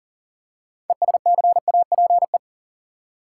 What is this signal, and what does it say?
Morse code, heard as a keyed tone, about 40 wpm, 700 Hz. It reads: ESCAPE
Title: Six Letter Words - 40wpm